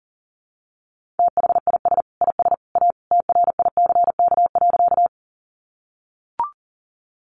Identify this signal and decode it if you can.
Morse code, heard as a keyed tone, about 40 wpm, 700 Hz. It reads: THIS IS A TRICK.
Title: Sentences from Top 1500 Words 40wpm